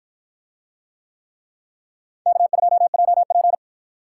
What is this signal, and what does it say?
Morse code, heard as a keyed tone, about 53 wpm, 700 Hz. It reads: K2JP